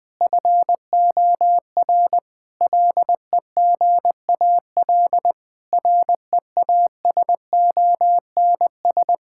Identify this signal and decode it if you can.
Morse code, heard as a keyed tone, about 21 wpm, 700 Hz. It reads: FOR LEGAL REASONS
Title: Sets of 8 Words - 20wpm